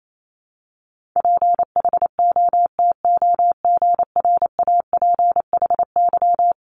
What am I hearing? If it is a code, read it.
Morse code, heard as a keyed tone, about 28 wpm, 700 Hz. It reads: PHOTOGRAPHY
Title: Multi-letter Phonemes Repeated 3x 28wpm